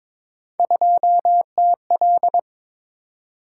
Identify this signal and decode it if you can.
Morse code, heard as a keyed tone, about 23 wpm, 700 Hz. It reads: 2TL
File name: practice / 22wpm / Three Letter-Number Combinations - 22wpm